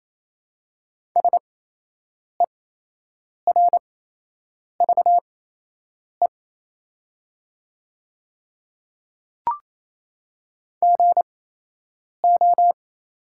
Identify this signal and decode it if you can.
Morse code, heard as a keyed tone, about 28 wpm, spaced as for 8 wpm, 700 Hz. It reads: SERVE GO
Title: 8x Character Spacing - Top 500 Words - 28wpm